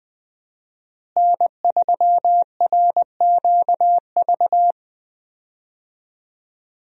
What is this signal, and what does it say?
Morse code, heard as a keyed tone, about 20 wpm, 700 Hz. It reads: N3RQV
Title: USA Call Signs - 5 Characters - Repeated 3x - 20wpm